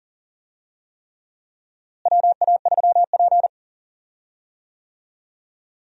Morse code, heard as a keyed tone, about 38 wpm, 700 Hz. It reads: WA3P
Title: USA Call Signs - 4 Characters - Repeated 3x - 40wpm